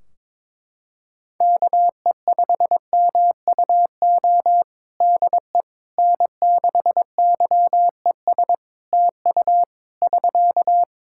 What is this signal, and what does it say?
Morse code, heard as a keyed tone, about 22 wpm, 700 Hz. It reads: KE5MUO DE N6YES TU <SK>
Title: K1USN SST QSO Practice 22wpm